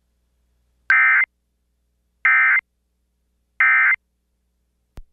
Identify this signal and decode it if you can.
Emergency Alert System (SAME) tones, heard as harsh buzzing bursts. It is heard from 0.9 to 3.9 s.